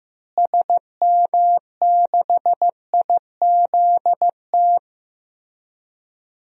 Morse code, heard as a keyed tone, about 16 wpm, 700 Hz. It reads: SM6IZT
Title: DXCC-100 Call Signs - 6 Characters - Repeated 3x - 15wpm